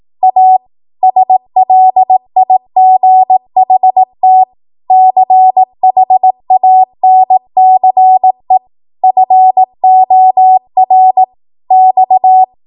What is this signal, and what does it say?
Morse code, heard as a keyed tone, about 18 wpm, 750 Hz. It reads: A SLIGHT CHANCE FOR X
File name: Morse / 18WPM